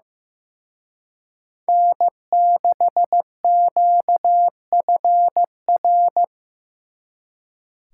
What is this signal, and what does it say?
Morse code, heard as a keyed tone, about 18 wpm, 700 Hz. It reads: N6QFR
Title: Speed Racing - USA 1x3 Call Signs 15wpm